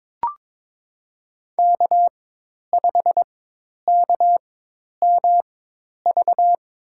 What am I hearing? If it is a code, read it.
Morse code, heard as a keyed tone, about 21 wpm, spaced as for 10 wpm, 700 Hz. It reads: K5KMV
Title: USA Call Signs - 5 Characters - 4x Character Spacing - 22wpm